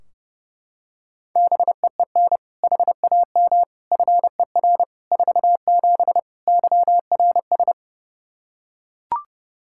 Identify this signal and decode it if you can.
Morse code, heard as a keyed tone, about 30 wpm, 700 Hz. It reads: BEEN HAM FER 47 YRS
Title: Practice QSO - Exchange 3 - More Info - 30wpm